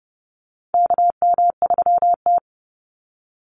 Morse code, heard as a keyed tone, about 28 wpm, 700 Hz. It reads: KM3T
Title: Call Signs Repeated 2x - 30wpm